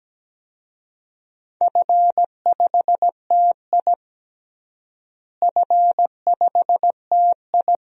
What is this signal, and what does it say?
Morse code, heard as a keyed tone, about 17 wpm, 700 Hz. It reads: F5TI F5TI
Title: DXCC-100 Call Signs - 4 Characters - Repeated 2x - 17wpm